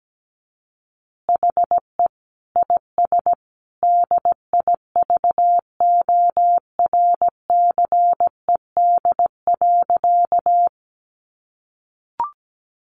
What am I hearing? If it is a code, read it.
Morse code, heard as a keyed tone, about 17 wpm, 700 Hz. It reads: HE IS DIVORCED.
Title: Additional Easy Phrases 17wpm